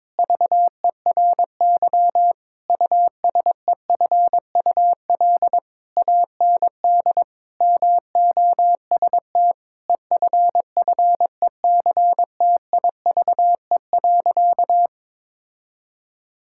Morse code, heard as a keyed tone, about 22 wpm, 700 Hz. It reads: VERY USEFUL AND MOST EFFECTIVE.
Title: Sentences from Top 1000 Words - Review Part 6 - 22wpm